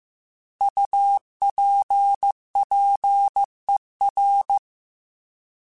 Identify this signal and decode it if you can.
Morse code, heard as a keyed tone, about 15 wpm, 800 Hz. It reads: UPPER